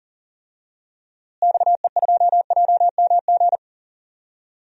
Morse code, heard as a keyed tone, about 40 wpm, 700 Hz. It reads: XE2JMG
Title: DXCC-100 Call Signs - 6 Characters - Repeated 2x - 40wpm